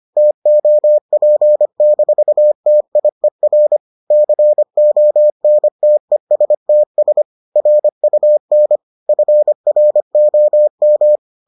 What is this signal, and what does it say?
Morse code, heard as a keyed tone, about 25 wpm, 600 Hz. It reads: TOP-TIER CONTESTS RUN FROM